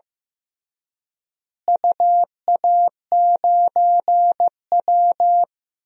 Morse code, heard as a keyed tone, about 14 wpm, 700 Hz. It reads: UA9W
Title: DXCC-100 Call Signs - 4 Characters - Repeated 2x - 15wpm